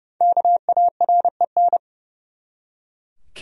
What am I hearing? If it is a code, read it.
Morse code, heard as a keyed tone, about 30 wpm, 700 Hz. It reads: KAREN
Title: USA Common Names - Max 5 Letters - 30wpm